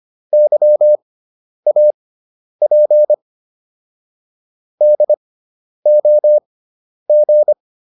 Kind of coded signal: Morse code

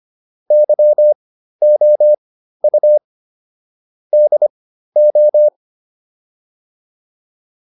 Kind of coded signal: Morse code

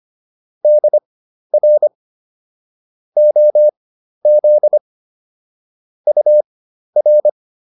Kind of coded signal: Morse code

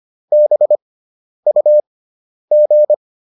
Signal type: Morse code

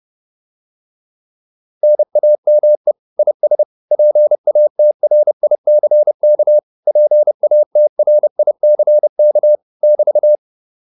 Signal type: Morse code

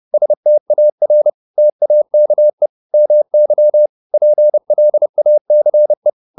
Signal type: Morse code

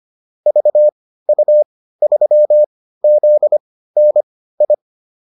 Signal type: Morse code

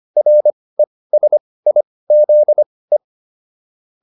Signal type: Morse code

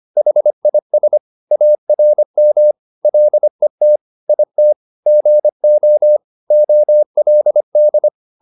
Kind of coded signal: Morse code

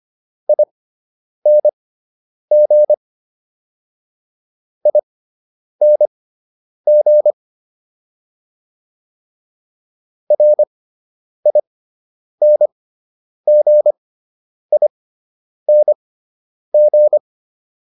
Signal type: Morse code

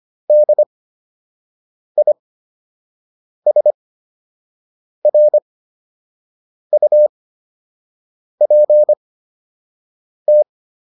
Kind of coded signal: Morse code

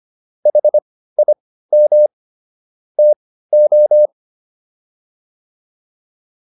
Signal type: Morse code

